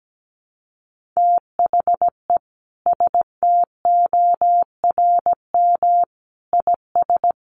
Morse code, heard as a keyed tone, about 17 wpm, 700 Hz. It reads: THE STORM IS
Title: Sentences from Top 1500 Words 17wpm